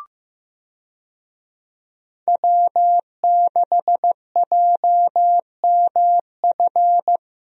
Morse code, heard as a keyed tone, about 15 wpm, 700 Hz. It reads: W6JMF
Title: USA Call Signs - 5 Characters - Repeated 3x - 15wpm